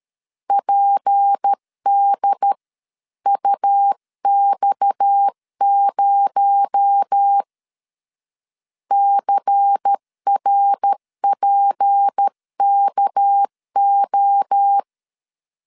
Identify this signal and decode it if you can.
Morse code, heard as a keyed tone, about 13 wpm, 800 Hz. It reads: PDUX0 CRPKO
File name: CW / DET / 1 6 1500C S C 12wpm 3533